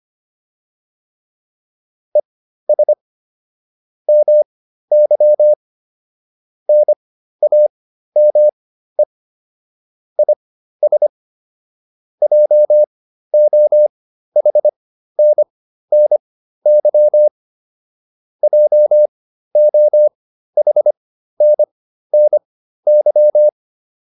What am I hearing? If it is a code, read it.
Morse code, heard as a keyed tone, about 25 wpm, spaced as for 13 wpm, 600 Hz. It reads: ES MY NAME IS JOHNNY JOHNNY